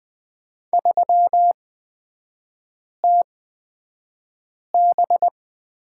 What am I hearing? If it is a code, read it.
Morse code, heard as a keyed tone, about 20 wpm, spaced as for 5 wpm, 700 Hz. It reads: 3TB
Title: 243 Lesson 39e - Letter z - Three Characters 20wpm